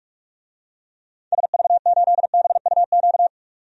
Morse code, heard as a keyed tone, about 44 wpm, 700 Hz. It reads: SV8BUQ
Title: DXCC-100 Call Signs - 6 Characters - 45wpm